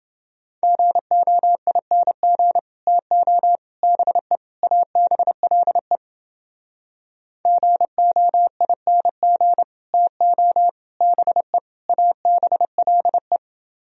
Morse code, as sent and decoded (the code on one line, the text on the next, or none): --. --- .. -. --. / - --- / -... / .- -... .-.. / --. --- .. -. --. / - --- / -... / .- -... .-.. .
GOING TO BE ABLE GOING TO BE ABLE